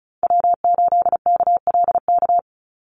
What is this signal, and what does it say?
Morse code, heard as a keyed tone, about 34 wpm, 700 Hz. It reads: W8KLK
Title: USA Call Signs 35wpm